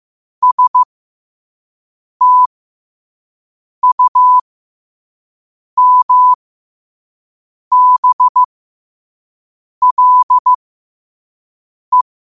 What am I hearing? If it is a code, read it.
Morse code, heard as a keyed tone, about 15 wpm, spaced as for 5 wpm, 1000 Hz. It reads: STUMBLE
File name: qotdfiles / 5wpm